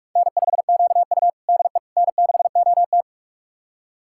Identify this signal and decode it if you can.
Morse code, heard as a keyed tone, about 46 wpm, 700 Hz. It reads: N5QU DE N6OT
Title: USA Callsign DE Callsign - 4 Characters - 45wpm